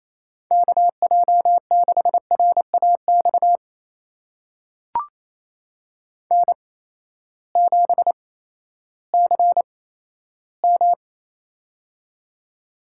Morse code, heard as a keyed tone, about 28 wpm, spaced as for 14 wpm, 700 Hz. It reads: KJ6RAX N7CM
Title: USA Call Signs - 8x Character Spacing - 28wpm